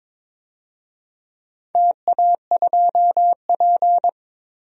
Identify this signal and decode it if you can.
Morse code, heard as a keyed tone, about 22 wpm, 700 Hz. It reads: TA2P
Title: DXCC-100 Call Signs - 4 Characters - 22wpm